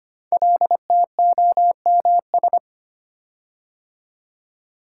Morse code, heard as a keyed tone, about 25 wpm, 700 Hz. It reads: LTOMS
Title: Code Groups - 5 Letter:Numbers - 25wpm